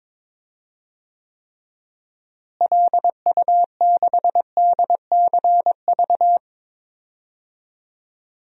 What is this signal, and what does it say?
Morse code, heard as a keyed tone, about 22 wpm, 700 Hz. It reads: LU6DCV
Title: DXCC-100 Call Signs - 6 Characters - Repeated 2x - 22wpm